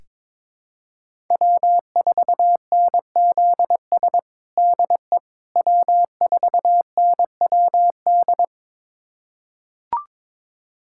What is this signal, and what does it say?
Morse code, heard as a keyed tone, about 22 wpm, 700 Hz. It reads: W4NZS DE W4NWD
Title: USA Callsign DE Callsign - 5 Characters - 22wpm